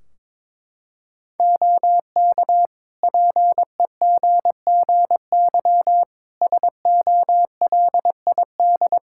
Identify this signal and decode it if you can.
Morse code, heard as a keyed tone, about 22 wpm, 700 Hz. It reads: OK PEGGY SOLID
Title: Practice QSO - Exchange 3 - More Info - 22wpm